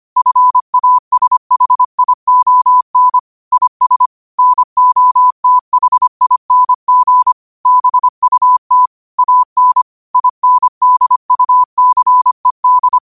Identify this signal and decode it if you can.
Morse code, heard as a keyed tone, about 25 wpm, 1000 Hz. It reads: FASHION IS NOTHING BUT AN INDUCED
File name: qotdfiles / 25wpm